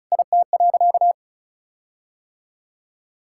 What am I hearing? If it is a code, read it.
Morse code, heard as a keyed tone, about 34 wpm, 700 Hz. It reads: IT.
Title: Sentences from Top 1600 Words 35wpm